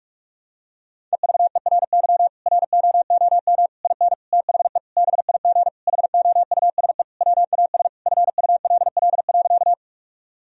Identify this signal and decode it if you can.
Morse code, heard as a keyed tone, about 46 wpm, 700 Hz. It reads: EVERY ROOM IN THE BIG HOUSE WAS FULL.
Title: Sentences from Top 1000 Words - Review Part 3 - 45wpm